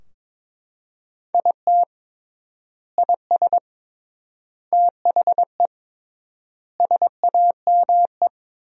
Morse code, heard as a keyed tone, about 22 wpm, 700 Hz. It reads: IT IS THE SAME
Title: 3x Word-Spacing - Sentences from Top 200 Words - 22wpm